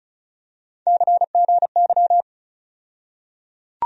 Morse code, heard as a keyed tone, about 34 wpm, 700 Hz. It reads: CGY
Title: USA Call Sign Postfixes - 3 Letters 35wpm